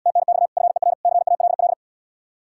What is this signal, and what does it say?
Morse code, heard as a keyed tone, about 79 wpm, 700 Hz. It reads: TNX FER CALL